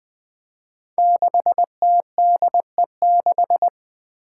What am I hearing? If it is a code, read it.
Morse code, heard as a keyed tone, about 20 wpm, 700 Hz. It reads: 6TDE6
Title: Five Challenging Letter-Number Combinations 20wpm